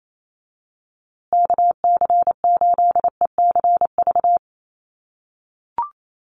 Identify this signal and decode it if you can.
Morse code, heard as a keyed tone, about 28 wpm, 700 Hz. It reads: KC8ECV
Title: USA Call Signs Repeated 2x 28wpm